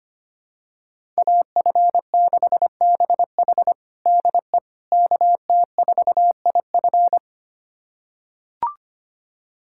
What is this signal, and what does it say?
Morse code, heard as a keyed tone, about 25 wpm, 700 Hz. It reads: AF6BH DE KT4IF
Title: USA Callsign DE Callsign - 5 Characters - 25wpm